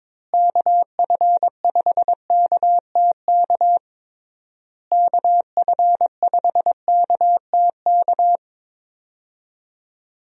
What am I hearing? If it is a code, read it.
Morse code, heard as a keyed tone, about 22 wpm, 700 Hz. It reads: KF5KTK KF5KTK
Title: USA Call Signs - 6 Characters - Repeated 3x - 22wpm